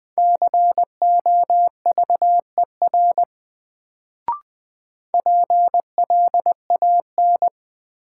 Morse code, heard as a keyed tone, about 20 wpm, 700 Hz. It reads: COVER PLAN
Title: Sentences from Top 500 Words - New Word Drill - 20wpm